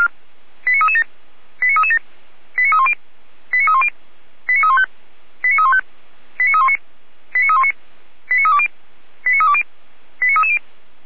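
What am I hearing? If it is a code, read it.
Five-tone selective call (ZVEI), heard as quick runs of five tones, tones of about 80 ms. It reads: 89337 89337 89310 89310 89315 89315 89319 89319 89320 89320 89330
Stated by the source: ZVEI five-tone call received by SDR, archived by Feuerwehr Orlamünde.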